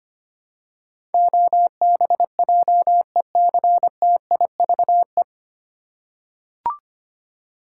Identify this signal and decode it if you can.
Morse code, heard as a keyed tone, about 25 wpm, 700 Hz. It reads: OBJECTIVE
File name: Nine Letter Words - 25wpm